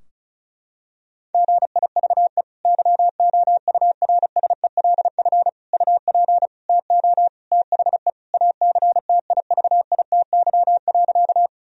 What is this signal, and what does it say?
Morse code, heard as a keyed tone, about 36 wpm, 700 Hz. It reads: GIVE YOURSELF UP TO THE ACTIVITY.
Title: Sentences from Top 1000 Words - Review Part 4 - 35wpm